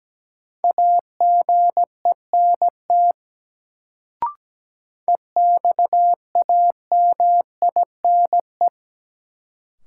Morse code, heard as a keyed tone, about 17 wpm, 700 Hz. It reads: AGENT EXAMINE
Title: Sentences from Top 1600 Words - New Word Drill - 17wpm